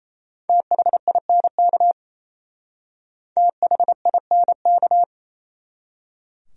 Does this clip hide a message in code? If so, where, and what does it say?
Yes: Morse code from 0.5 to 5.0 s, heard as a keyed tone, about 35 wpm, 700 Hz. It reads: THINK THINK